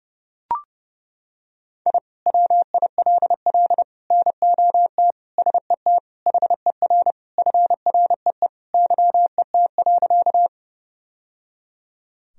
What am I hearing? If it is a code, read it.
Morse code, heard as a keyed tone, about 30 wpm, 700 Hz. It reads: I WILL NOT SET HER FREE YET.
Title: Sentences from Top 1000 Words - Review Part 1 - 30wpm